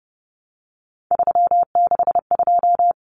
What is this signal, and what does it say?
Morse code, heard as a keyed tone, about 29 wpm, 700 Hz. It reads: 362
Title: Common Cut Numbers - 3-Digit 30wpm